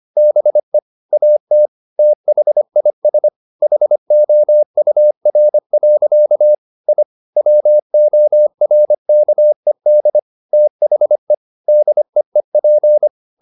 Morse code, heard as a keyed tone, about 25 wpm, 600 Hz. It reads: BE AT THIS HOUR. I WORKED THE DEEP
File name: the old man and the sea part 1 25wpm